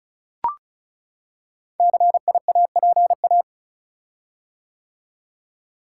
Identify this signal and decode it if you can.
Morse code, heard as a keyed tone, about 36 wpm, 700 Hz. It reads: CIAPA
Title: Code Groups - 5 Letters - 35wpm